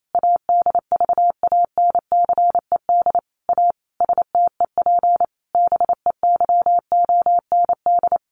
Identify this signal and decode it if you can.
Morse code, heard as a keyed tone, about 28 wpm, 700 Hz. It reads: ADVANCED A STEP BEYOND
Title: Sentences from Top 1200 Words 28wpm